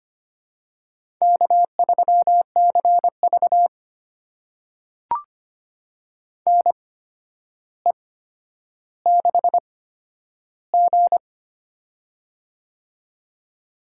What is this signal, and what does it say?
Morse code, heard as a keyed tone, about 25 wpm, spaced as for 11 wpm, 700 Hz. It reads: K3CV NE6G
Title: USA Call Signs - 4 Characters - 8x Character Spacing - 25wpm